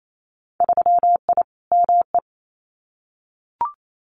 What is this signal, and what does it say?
Morse code, heard as a keyed tone, about 28 wpm, 700 Hz. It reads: 3I ME
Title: Winter Field Day Exchanges 28wpm